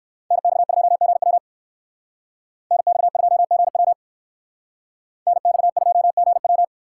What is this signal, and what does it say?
Morse code, heard as a keyed tone, about 58 wpm, 700 Hz. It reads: NX2GW NX2GW NX2GW